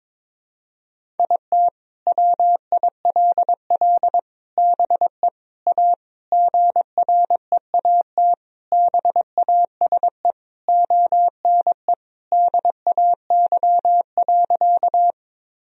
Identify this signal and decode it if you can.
Morse code, heard as a keyed tone, about 21 wpm, 700 Hz. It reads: IT WILL BE A GREAT BASE ONE DAY.
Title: Sentences from Top 1000 Words - Review Part 1 - 22wpm